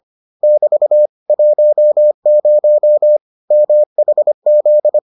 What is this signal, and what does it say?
Morse code, heard as a keyed tone, about 25 wpm, 600 Hz. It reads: <BT> 10 MHZ